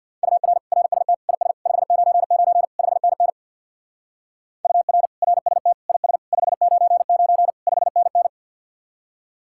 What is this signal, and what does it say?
Morse code, heard as a keyed tone, about 60 wpm, 700 Hz. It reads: UR RST IS 599 5NN UR RST IS 599 5NN
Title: Speed-Racing Common QSO Phrases - 50wpm